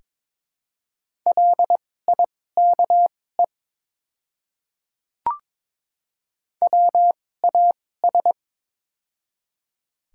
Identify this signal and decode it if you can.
Morse code, heard as a keyed tone, about 22 wpm, spaced as for 16 wpm, 690 Hz. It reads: LIKE WAS
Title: 2x Character Spacing - Top 100 Words - 22wpm